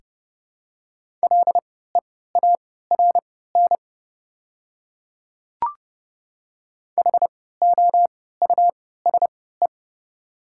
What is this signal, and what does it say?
Morse code, heard as a keyed tone, about 32 wpm, spaced as for 17 wpm, 700 Hz. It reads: LEARN HOUSE
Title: 3x Character Spacing - Top 500 Words - 30wpm